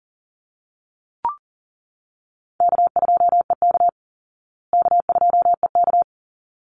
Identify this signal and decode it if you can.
Morse code, heard as a keyed tone, about 40 wpm, 700 Hz. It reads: K2EK K2EK
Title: Easy Call Signs Repeated 2x - 40wpm